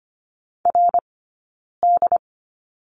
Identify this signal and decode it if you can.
Morse code, heard as a keyed tone, about 25 wpm, spaced as for 9 wpm, 690 Hz. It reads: RD
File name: Lesson 15d - Number 5 - Two Characters 25wpm